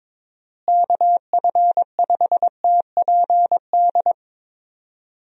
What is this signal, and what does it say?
Morse code, heard as a keyed tone, about 21 wpm, 700 Hz. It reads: KF5TPD
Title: USA Call Signs - 6 Characters - Repeated 2x - 22wpm